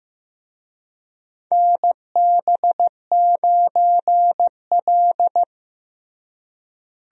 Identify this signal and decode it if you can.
Morse code, heard as a keyed tone, about 15 wpm, 700 Hz. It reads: NB9L
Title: USA Call Signs - 4 Characters - Repeated 2x - 15wpm